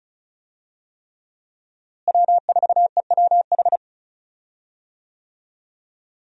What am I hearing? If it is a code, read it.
Morse code, heard as a keyed tone, about 34 wpm, 700 Hz. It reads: W4EWH
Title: USA Call Signs - 5 Characters - Repeated 3x - 35wpm